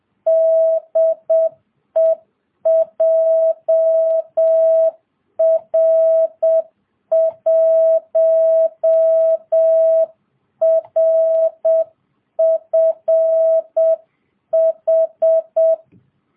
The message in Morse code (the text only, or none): DE JR1RFH